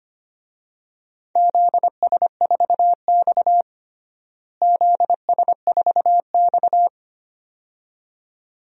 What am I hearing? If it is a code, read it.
Morse code, heard as a keyed tone, about 25 wpm, 700 Hz. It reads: ZS4X ZS4X
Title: DXCC-100 Call Signs - 4 Characters - Repeated 2x - 25wpm